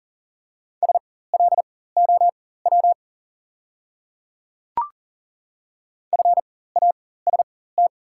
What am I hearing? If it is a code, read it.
Morse code, heard as a keyed tone, about 38 wpm, spaced as for 18 wpm, 690 Hz. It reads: SLOW FAST
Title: 4x Character Spacing - Top 500 Words - 40wpm